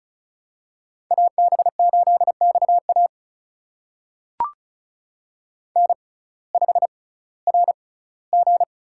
Morse code, heard as a keyed tone, about 36 wpm, spaced as for 19 wpm, 700 Hz. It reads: AB8XA N5RG
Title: USA Call Signs - 5 Characters - 6x Character Spacing - 35wpm